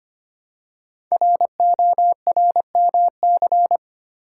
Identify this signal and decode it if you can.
Morse code, heard as a keyed tone, about 25 wpm, 700 Hz. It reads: RORMC
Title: Code Groups - 5 Letters - 25wpm